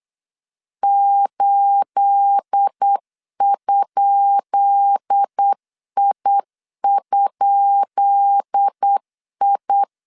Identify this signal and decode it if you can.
Morse code, heard as a keyed tone, about 8 wpm, 800 Hz. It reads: 8?I?I
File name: LEZIONI / 10 SIGNS / 10B INT 8wpm 3534